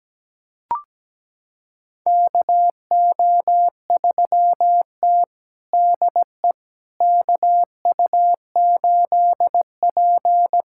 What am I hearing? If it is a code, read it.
Morse code, heard as a keyed tone, about 17 wpm, 700 Hz. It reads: KO3T DE KU8P
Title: USA Callsign DE Callsign - 4 Characters - 17wpm